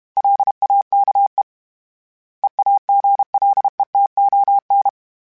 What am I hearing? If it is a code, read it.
Morse code, heard as a keyed tone, about 32 wpm, 800 Hz. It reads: LAKE EAGLETON
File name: fromSMs / FromSMs mp3s / Cities@30